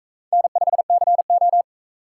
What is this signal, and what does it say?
Morse code, heard as a keyed tone, about 40 wpm, 700 Hz. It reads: N5CO